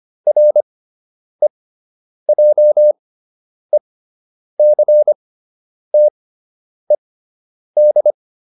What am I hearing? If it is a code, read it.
Morse code, heard as a keyed tone, about 25 wpm, spaced as for 9 wpm, 600 Hz. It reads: REJECTED